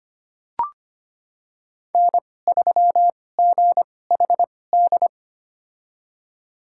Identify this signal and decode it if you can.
Morse code, heard as a keyed tone, about 25 wpm, spaced as for 18 wpm, 700 Hz. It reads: N3GHD
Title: USA Call Signs - 2x Character Spacing - 25wpm